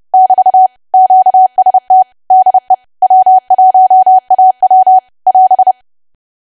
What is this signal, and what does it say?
Morse code, heard as a keyed tone, about 30 wpm, 750 Hz. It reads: <BT> QST DE W1AW <AS>
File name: cwa / W1AW-Level3 / 30WPM